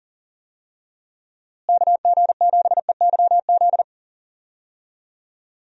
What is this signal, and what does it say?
Morse code, heard as a keyed tone, about 40 wpm, 700 Hz. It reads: KG7EYZ
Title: USA Call Signs - 6 Characters - Repeated 3x - 40wpm